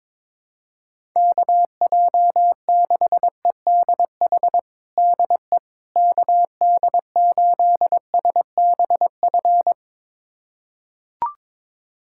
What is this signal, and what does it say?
Morse code, heard as a keyed tone, about 22 wpm, 700 Hz. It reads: KJ6EDH DE KD8SBF
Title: USA Callsign DE Callsign - 6 Characters - 22wpm